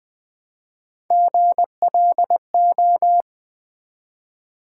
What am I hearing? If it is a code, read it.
Morse code, heard as a keyed tone, about 20 wpm, 700 Hz. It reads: GLO
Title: Build A Word - 1 Letter At A Time - 5-Letter Words - 20wpm